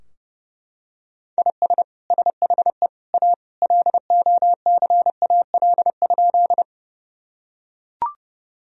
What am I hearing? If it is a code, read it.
Morse code, heard as a keyed tone, about 31 wpm, 700 Hz. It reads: IS SHE A LOCAL?
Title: Sentences from Top 1000 Words - Review Part 3 - 30wpm